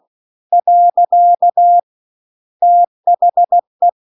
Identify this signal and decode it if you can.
Morse code, heard as a keyed tone, about 16 wpm, 700 Hz. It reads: . THE